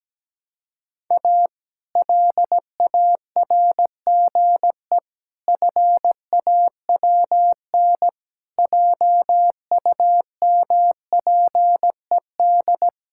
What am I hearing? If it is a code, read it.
Morse code, heard as a keyed tone, about 17 wpm, 700 Hz. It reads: A LARGE FAWN JUMPED
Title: Pangram Challenge v2 - 17wpm